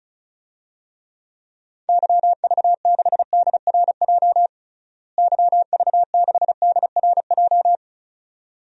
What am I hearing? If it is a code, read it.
Morse code, heard as a keyed tone, about 36 wpm, 700 Hz. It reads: YV6DRJ YV6DRJ
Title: DXCC-100 Call Signs - 6 Characters - Repeated 2x - 35wpm